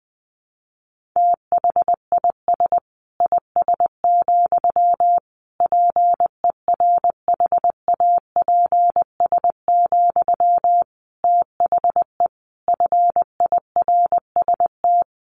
Morse code, heard as a keyed tone, about 20 wpm, 700 Hz. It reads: THIS IS, PERHAPS, THE FIRST
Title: Sentences from Top 1500 Words 20wpm